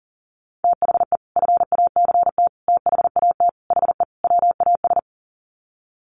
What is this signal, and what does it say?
Morse code, heard as a keyed tone, about 40 wpm, 700 Hz. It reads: THE FACT THAT HE WAS
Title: Sets of 5 Words 40wpm